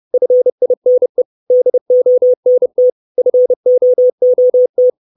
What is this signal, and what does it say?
Morse code, heard as a keyed tone, about 30 wpm, 500 Hz. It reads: FINE DONT FOOT
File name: 632 Words 30WPM